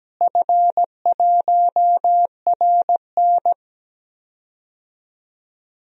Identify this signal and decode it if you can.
Morse code, heard as a keyed tone, about 17 wpm, 700 Hz. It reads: F1RN
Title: DXCC-100 Call Signs - 4 Characters - Repeated 2x - 17wpm